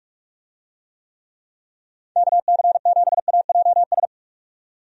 Morse code, heard as a keyed tone, about 46 wpm, 700 Hz. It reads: KC7AJS
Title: USA Call Signs - 6 Characters - 45wpm